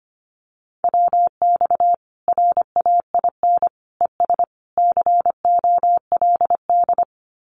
Morse code, heard as a keyed tone, about 25 wpm, 700 Hz. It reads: WX RAIN ES COLD
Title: Common QSO Phrases 25wpm